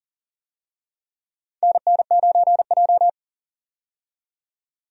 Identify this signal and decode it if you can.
Morse code, heard as a keyed tone, about 40 wpm, 700 Hz. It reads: NN9J
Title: USA Call Signs - 4 Characters - 40wpm